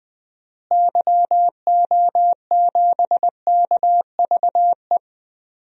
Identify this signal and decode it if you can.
Morse code, heard as a keyed tone, about 20 wpm, 700 Hz. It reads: YO7KVE
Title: DXCC-100 Call Signs - 6 Characters - Repeated 3x - 20wpm